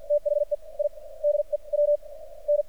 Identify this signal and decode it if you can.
Morse code, heard as a keyed tone, about 49 wpm, spaced as for 33 wpm, 590 Hz. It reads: THE IDE W N
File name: archived / CRSnet